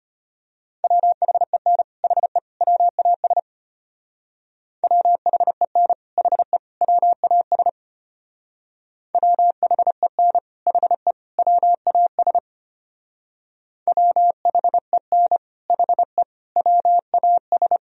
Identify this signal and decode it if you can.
Morse code, heard as a keyed tone, about 30 wpm, 700 Hz. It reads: WHEN HE WAS WHEN HE WAS WHEN HE WAS WHEN HE WAS
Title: Speed-Racing - Sets of 3 Words 25wpm